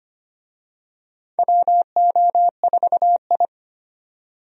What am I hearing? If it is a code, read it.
Morse code, heard as a keyed tone, about 25 wpm, 700 Hz. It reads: WO4I
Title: USA Call Signs - 4 Characters - 25wpm